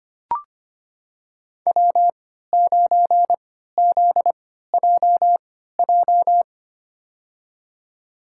Morse code, heard as a keyed tone, about 25 wpm, spaced as for 14 wpm, 700 Hz. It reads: W9ZJJ
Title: USA Call Signs - 5 Characters - 3x Character Spacing - 25wpm